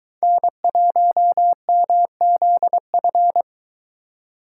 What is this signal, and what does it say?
Morse code, heard as a keyed tone, about 22 wpm, 700 Hz. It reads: N1MZF